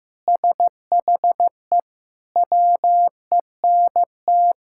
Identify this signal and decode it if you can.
Morse code, heard as a keyed tone, about 15 wpm, 700 Hz. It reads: SHE WENT
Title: Sentences from Top 1000 Words - Review Part 6 - 15wpm